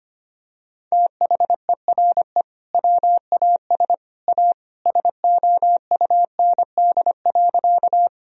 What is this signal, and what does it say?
Morse code, heard as a keyed tone, about 25 wpm, 700 Hz. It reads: THERE WAS A SOUND.
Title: Sentences from Top 1000 Words - Review Part 1 - 25wpm